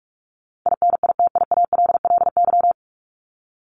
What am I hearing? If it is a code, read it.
Morse code, heard as a keyed tone, about 44 wpm, 700 Hz. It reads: INITIALLY
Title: Top 1251-1500 Words 45wpm